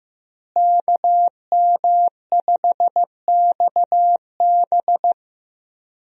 Morse code, heard as a keyed tone, about 15 wpm, 700 Hz. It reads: KM5XB
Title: USA Call Signs - 5 Characters - Repeated 3x - 15wpm